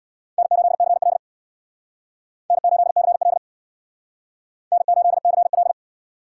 Morse code, heard as a keyed tone, about 62 wpm, 700 Hz. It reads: N8CL N8CL N8CL
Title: Speed Racing - USA Call Signs 50wpm